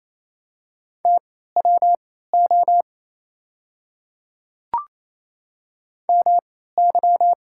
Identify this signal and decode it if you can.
Morse code, heard as a keyed tone, about 28 wpm, spaced as for 16 wpm, 690 Hz. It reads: TWO MY
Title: 3x Character Spacing - Top 100 Words - 28wpm